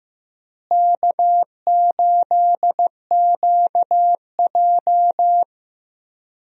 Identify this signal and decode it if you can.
Morse code, heard as a keyed tone, about 15 wpm, 700 Hz. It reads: K8QJ
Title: USA Call Signs - 4 Characters - Repeated 3x - 15wpm